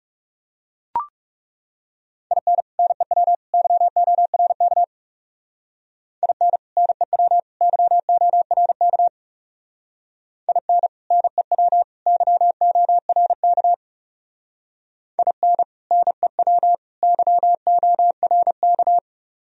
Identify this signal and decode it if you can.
Morse code, heard as a keyed tone, about 37 wpm, 700 Hz. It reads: IN NEW YORK IN NEW YORK IN NEW YORK IN NEW YORK
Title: Speed-Racing - Sets of 3 Words 30wpm